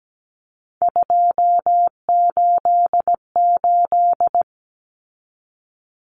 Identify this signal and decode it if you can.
Morse code, heard as a keyed tone, about 17 wpm, 700 Hz. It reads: 288
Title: Common Cut Numbers - 3-Digit 17wpm